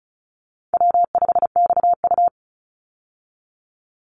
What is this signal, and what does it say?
Morse code, heard as a keyed tone, about 34 wpm, 700 Hz. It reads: W5XU
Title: Easy Call Signs Repeated 2x - 35wpm